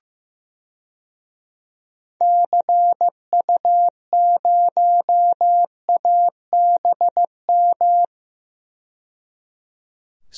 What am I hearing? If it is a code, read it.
Morse code, heard as a keyed tone, about 15 wpm, 700 Hz. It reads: CU0ABM
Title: DXCC-100 Call Signs - 6 Characters - 15wpm